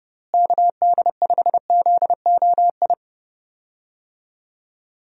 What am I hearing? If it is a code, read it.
Morse code, heard as a keyed tone, about 31 wpm, 700 Hz. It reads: KD5ZOI
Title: USA Call Signs - 6 Characters - Repeated 3x - 30wpm